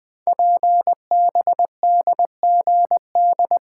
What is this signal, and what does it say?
Morse code, heard as a keyed tone, about 20 wpm, 700 Hz. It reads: PBDGD
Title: Code Groups - 5 Letters - 20wpm